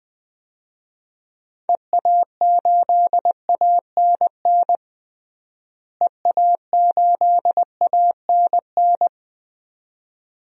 Morse code, heard as a keyed tone, about 21 wpm, 700 Hz. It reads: EA8ANN EA8ANN
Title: DXCC-100 Call Signs - 6 Characters - Repeated 2x - 20wpm